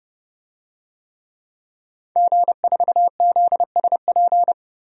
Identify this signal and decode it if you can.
Morse code, heard as a keyed tone, about 30 wpm, 700 Hz. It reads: G4ZSP
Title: DXCC-100 Call Signs - 5 Characters - 30wpm